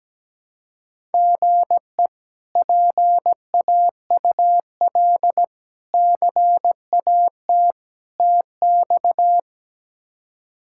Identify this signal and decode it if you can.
Morse code, heard as a keyed tone, about 17 wpm, 700 Hz. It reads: GE PAUL CAT TX
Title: K1USN SST QSO Practice - 17wpm